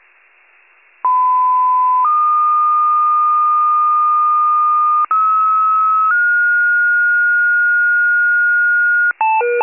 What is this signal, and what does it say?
Two-tone page, heard as a one-second tone, then a three-second tone, about 1010 then 1220 Hz, about 1290 then 1430 Hz. It is heard from 1.0 to 9.1 s.
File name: wcalert